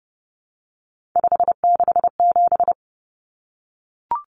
Morse code, heard as a keyed tone, about 30 wpm, 700 Hz. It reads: 567
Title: Numbers - 3-Digit 30wpm